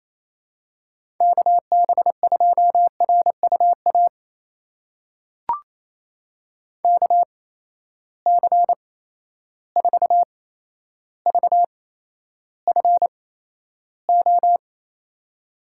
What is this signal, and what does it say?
Morse code, heard as a keyed tone, about 28 wpm, spaced as for 12 wpm, 700 Hz. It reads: KB2RUA KC4VFO
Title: USA Call Signs - 8x Character Spacing - 28wpm